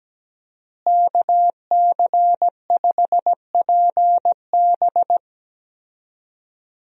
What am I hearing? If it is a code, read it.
Morse code, heard as a keyed tone, about 17 wpm, 700 Hz. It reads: KC5PB
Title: USA Call Signs - 5 Characters - Repeated 3x - 17wpm